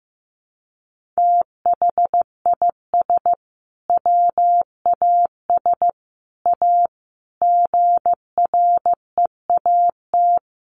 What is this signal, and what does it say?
Morse code, heard as a keyed tone, about 15 wpm, 700 Hz. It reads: THIS WAS A GREAT
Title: Sentences from Top 1400 Words 15wpm